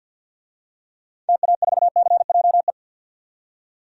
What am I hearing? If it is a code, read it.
Morse code, heard as a keyed tone, about 46 wpm, 700 Hz. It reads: TA4CJE
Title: DXCC-100 Call Signs - 6 Characters - 50wpm